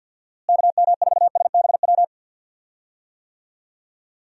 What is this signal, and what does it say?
Morse code, heard as a keyed tone, about 52 wpm, 700 Hz. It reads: KM4SBW
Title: USA Call Signs - 6 Characters - Repeated 3x - 50wpm